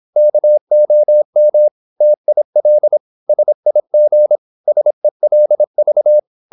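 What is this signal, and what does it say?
Morse code, heard as a keyed tone, about 26 wpm, 600 Hz. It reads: KOM TIL SIG SELV
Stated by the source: High-speed (QRQ) Morse at 26 wpm.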